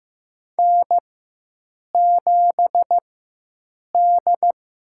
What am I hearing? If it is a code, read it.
Morse code, heard as a keyed tone, about 15 wpm, spaced as for 7 wpm, 700 Hz. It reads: N7D
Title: USA Call Signs - 4x Character Spacing - 15wpm